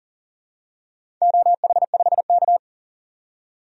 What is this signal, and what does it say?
Morse code, heard as a keyed tone, about 37 wpm, 700 Hz. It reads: OH5K
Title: DXCC-100 Call Signs - 4 Characters - 40wpm